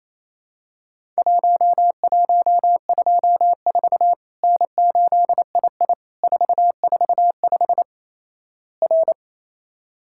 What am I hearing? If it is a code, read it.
Morse code, heard as a keyed tone, about 28 wpm, 700 Hz. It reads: 1124 N8II 445 R
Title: Worked All Europe DX Contest - QTC Exchange - 28wpm